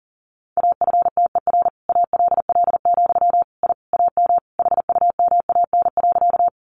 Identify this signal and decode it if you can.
Morse code, heard as a keyed tone, about 40 wpm, 700 Hz. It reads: AFTER ALL, I AM HUMAN.
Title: Sentences from Top 600 Words 40wpm